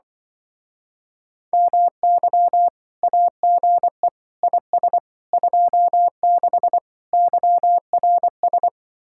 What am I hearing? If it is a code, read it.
Morse code, heard as a keyed tone, about 24 wpm, 700 Hz. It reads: MY AGE IS 26 YRS
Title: Speed-Racing Common QSO Phrases Encore - 20wpm